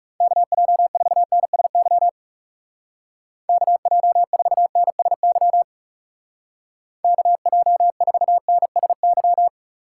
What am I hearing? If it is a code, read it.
Morse code, heard as a keyed tone, about 36 wpm, 700 Hz. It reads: KJ4NSY KJ4NSY KJ4NSY
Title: Speed Racing - USA Call Signs 30wpm